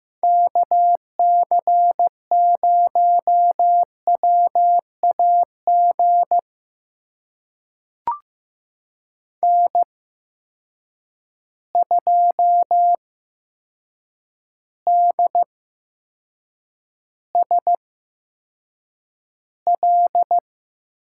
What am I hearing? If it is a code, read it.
Morse code, heard as a keyed tone, about 15 wpm, spaced as for 7 wpm, 700 Hz. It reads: KC0WAG N2DSL
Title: USA Call Signs - 8x Character Spacing - 15wpm